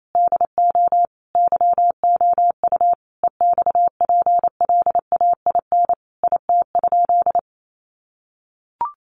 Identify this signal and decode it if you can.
Morse code, heard as a keyed tone, about 28 wpm, 700 Hz. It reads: DO YOU EXPLAIN IT?
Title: Sentences from Top 1100 Words 28wpm